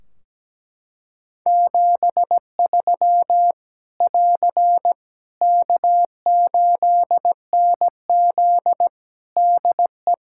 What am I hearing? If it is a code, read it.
Morse code, heard as a keyed tone, about 17 wpm, 700 Hz. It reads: 73 <AR> K8NZ DE
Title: Practice QSO - Exchange 4 - Ending - 17wpm